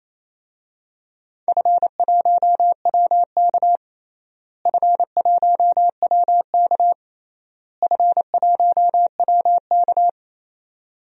Morse code, heard as a keyed tone, about 28 wpm, 700 Hz. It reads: F1WK F1WK F1WK
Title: DXCC-100 Call Signs - 4 Characters - Repeated 3x - 28wpm